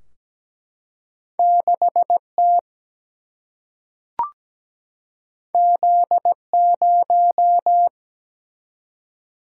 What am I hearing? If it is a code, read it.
Morse code, heard as a keyed tone, about 17 wpm, 700 Hz. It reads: 6 T Z 0
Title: Single-Letter Single-Number 17wpm